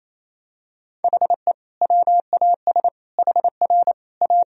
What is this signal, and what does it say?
Morse code, heard as a keyed tone, about 28 wpm, 700 Hz. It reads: HE WAS HR A
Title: Sentences from Top 500 Words and Abbreviations 28wpm